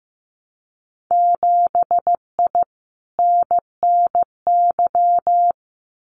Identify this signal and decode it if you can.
Morse code, heard as a keyed tone, about 15 wpm, 700 Hz. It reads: 7I NNY
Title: Winter Field Day Exchanges Encore 15wpm